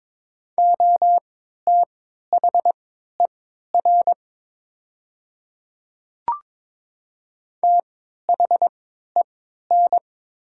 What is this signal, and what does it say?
Morse code, heard as a keyed tone, about 22 wpm, spaced as for 12 wpm, 700 Hz. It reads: OTHER THEN
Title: 3x Character Spacing - Top 200 Words - 22wpm